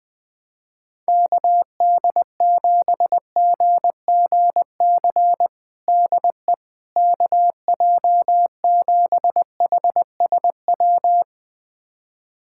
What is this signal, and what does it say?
Morse code, heard as a keyed tone, about 20 wpm, 700 Hz. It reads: KD7GGC DE KJ7HSW
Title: USA Callsign DE Callsign - 6 Characters - 20wpm